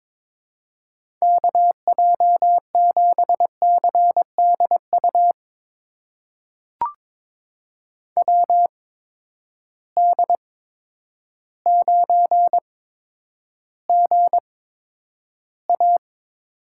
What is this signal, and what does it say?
Morse code, heard as a keyed tone, about 22 wpm, spaced as for 10 wpm, 700 Hz. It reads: KJ7CDU WD9GA
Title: USA Call Signs - 6 Characters - 8x Character Spacing - 22wpm